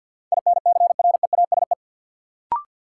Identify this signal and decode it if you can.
Morse code, heard as a keyed tone, about 52 wpm, 700 Hz. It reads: INCREASE
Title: Sets of 5 Words - New Word Drill - 50wpm